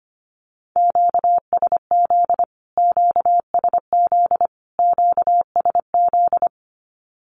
Morse code, heard as a keyed tone, about 25 wpm, 700 Hz. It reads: QSZ QSZ QSZ
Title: Q-Codes Repeated 3x 25wpm